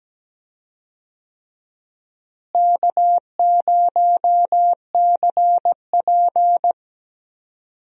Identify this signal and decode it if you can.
Morse code, heard as a keyed tone, about 17 wpm, 700 Hz. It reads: K0CP
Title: USA Call Signs - 4 Characters - Repeated 3x - 17wpm